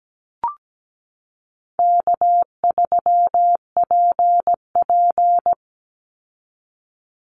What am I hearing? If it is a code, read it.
Morse code, heard as a keyed tone, about 17 wpm, 700 Hz. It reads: K3PP
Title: Easy Call Signs Repeated 2x - 17wpm